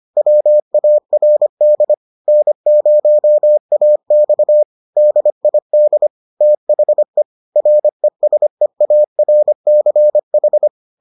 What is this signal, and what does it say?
Morse code, heard as a keyed tone, about 25 wpm, 600 Hz. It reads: WARD N0AX DID THE RESEARCH